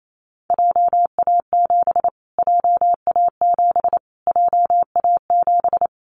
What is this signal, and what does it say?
Morse code, heard as a keyed tone, about 28 wpm, 700 Hz. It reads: JA7 JA7 JA7
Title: Callsign Prefixes for Top 25 DXCC - Repeated 3x - 28wpm